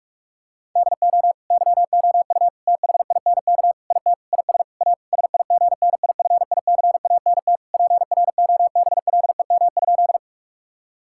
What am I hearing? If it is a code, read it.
Morse code, heard as a keyed tone, about 46 wpm, 700 Hz. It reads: DO YOU THINK IT IS A SIGNIFICANT PROBLEM?
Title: Sentences from Top 1000 Words - Review Part 6 - 45wpm